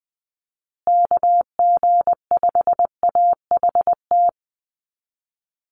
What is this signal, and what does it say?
Morse code, heard as a keyed tone, about 20 wpm, 700 Hz. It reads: KG5AHT
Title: USA Call Signs Repeated 3x 20wpm